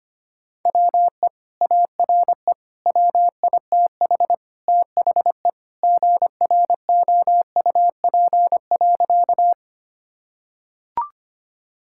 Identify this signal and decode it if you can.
Morse code, heard as a keyed tone, about 25 wpm, 700 Hz. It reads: WE ARE WITH THE GROUP.
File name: practice / 25wpm / Sentences with 2-Word Phrases - 25wpm